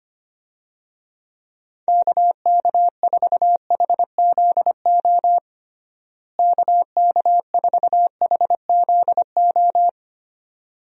Morse code, heard as a keyed tone, about 25 wpm, 700 Hz. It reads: KK4HZO KK4HZO
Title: USA Call Signs - 6 Characters - Repeated 2x - 25wpm